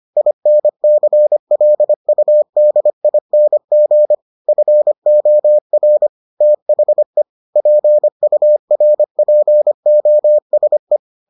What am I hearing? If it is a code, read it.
Morse code, heard as a keyed tone, about 25 wpm, 600 Hz. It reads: INCLUDING FOR THE PURPOSE